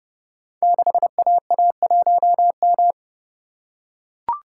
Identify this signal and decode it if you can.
Morse code, heard as a keyed tone, about 30 wpm, 700 Hz. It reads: 6AA1M
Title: Code Groups - 5 Letter:Numbers - 30wpm